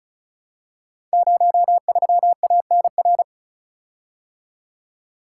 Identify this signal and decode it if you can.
Morse code, heard as a keyed tone, about 36 wpm, 700 Hz. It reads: 03ANR
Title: Code Groups - 5 Letter:Numbers - 35wpm